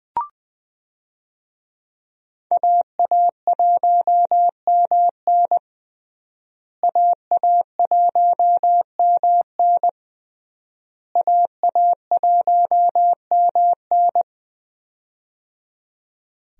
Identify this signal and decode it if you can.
Morse code, heard as a keyed tone, about 21 wpm, 700 Hz. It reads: AA1MN AA1MN AA1MN
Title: USA Call Signs - 5 Characters - Repeated 3x - 20wpm